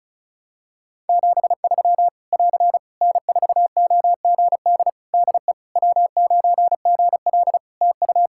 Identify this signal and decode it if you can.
Morse code, heard as a keyed tone, about 36 wpm, 700 Hz. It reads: 73 <AR> N4OGD DE W9GL TU
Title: K1USN SST QSO Practice 35wpm